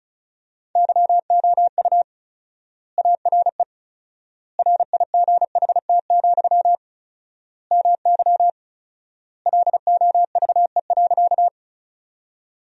Morse code, heard as a keyed tone, about 36 wpm, 700 Hz. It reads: YOU ARE RIGHT, MY LOVE.
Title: 4x Word-Spacing - Sentences from Top 400 Words - 35wpm